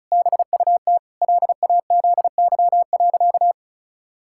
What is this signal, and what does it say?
Morse code, heard as a keyed tone, about 36 wpm, 700 Hz. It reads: BUT LAZY.
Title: Pangram Challenge v2 - 35wpm